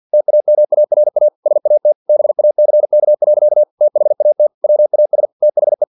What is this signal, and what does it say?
Morse code, heard as a keyed tone, about 50 wpm, 600 Hz. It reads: TAMARA SAT BACK. THAT WAS THE
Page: Never 49wpm